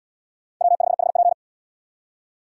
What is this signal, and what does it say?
Morse code, heard as a keyed tone, about 72 wpm, 700 Hz. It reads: W5HP